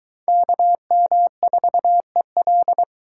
Morse code, heard as a keyed tone, about 22 wpm, 700 Hz. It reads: KM4EL